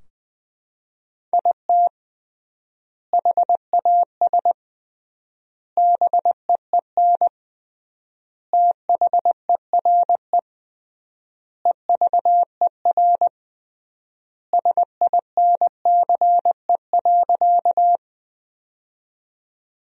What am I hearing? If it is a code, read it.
Morse code, heard as a keyed tone, about 20 wpm, 700 Hz. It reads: IT HAS BEEN THERE EVER SINCE.
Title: 3x Word-Spacing - Sentences from Top 400 Words - 20wpm